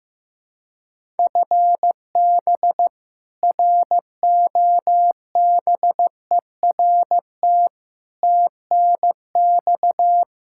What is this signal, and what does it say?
Morse code, heard as a keyed tone, about 15 wpm, 700 Hz. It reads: FB ROBERT TNX
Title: K1USN SST QSO Practice 15wpm